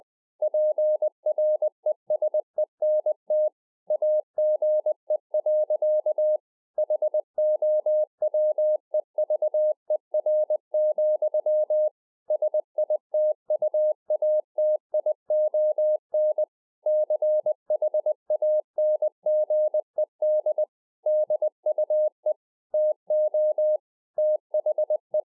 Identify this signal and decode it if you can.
Morse code, heard as a keyed tone, about 20 wpm, 610 Hz. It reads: PRESENT AGE. HOWEVER, SITUATION CHANGED DUE TO THE
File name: 20WPM